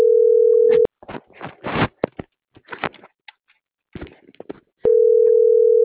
Ringback tone (North America), 440 + 480 Hz.
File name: CENTRAL NMP / out marisol.monzalvonmp